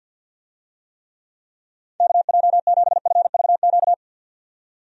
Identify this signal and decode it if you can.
Morse code, heard as a keyed tone, about 50 wpm, 700 Hz. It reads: KJ7FVQ
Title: USA Call Signs - 6 Characters - 50wpm